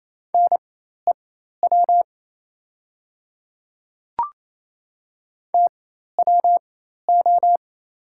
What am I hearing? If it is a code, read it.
Morse code, heard as a keyed tone, about 28 wpm, spaced as for 13 wpm, 700 Hz. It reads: NEW TWO
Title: 4x Character Spacing - Top 300 Words - 28wpm